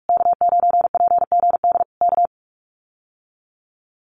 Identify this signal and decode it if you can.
Morse code, heard as a keyed tone, about 44 wpm, 700 Hz. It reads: K9PGD K
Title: Common QSO Phrases Encore 45wpm